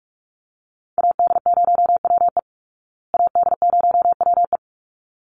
Morse code, heard as a keyed tone, about 44 wpm, 700 Hz. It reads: AD0WE AD0WE
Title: Call Signs Repeated 3x - 45wpm